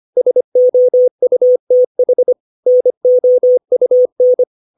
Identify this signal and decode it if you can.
Morse code, heard as a keyed tone, about 25 wpm, 500 Hz. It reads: SOUTH NOUN
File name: 632 words / 632 Words 25WPM